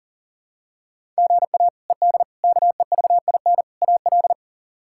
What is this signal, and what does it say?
Morse code, heard as a keyed tone, about 40 wpm, 700 Hz. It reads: GA ED KEVIN AL
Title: K1USN SST QSO Practice - 40wpm